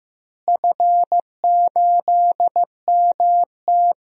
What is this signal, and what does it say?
Morse code, heard as a keyed tone, about 15 wpm, 700 Hz. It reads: F8MT
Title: DXCC-100 Call Signs - 4 Characters - Repeated 2x - 15wpm